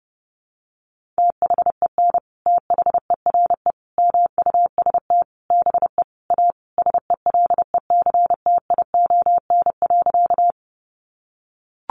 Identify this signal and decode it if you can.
Morse code, heard as a keyed tone, about 30 wpm, 700 Hz. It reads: THEN THERE MUST BE A SELECTION.
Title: Sentences from Top 1400 Words 30wpm